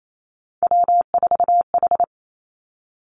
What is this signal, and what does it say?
Morse code, heard as a keyed tone, about 28 wpm, 700 Hz. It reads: W4H
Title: Three Challenging Letter-Number Combinations 28wpm